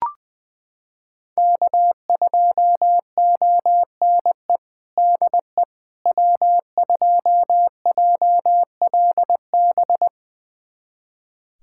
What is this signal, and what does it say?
Morse code, heard as a keyed tone, about 20 wpm, 700 Hz. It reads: K2ONE DE W2JLB
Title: USA Callsign DE Callsign - 5 Characters - 20wpm